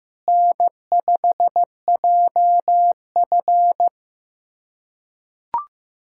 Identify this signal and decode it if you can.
Morse code, heard as a keyed tone, about 15 wpm, 700 Hz. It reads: N5JF
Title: USA Callsign DE Callsign - 4 Characters - 15wpm